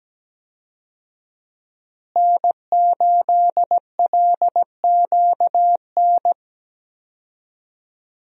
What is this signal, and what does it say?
Morse code, heard as a keyed tone, about 17 wpm, 700 Hz. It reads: N8LQN
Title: USA Call Signs - 5 Characters - Repeated 2x - 17wpm